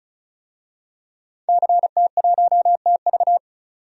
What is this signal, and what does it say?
Morse code, heard as a keyed tone, about 36 wpm, 700 Hz. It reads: CT1TV
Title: DXCC-100 Call Signs - 5 Characters - 35wpm